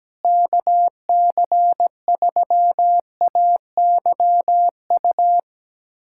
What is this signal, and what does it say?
Morse code, heard as a keyed tone, about 17 wpm, 700 Hz. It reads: KC3AYU
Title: USA Callsign DE Callsign 17wpm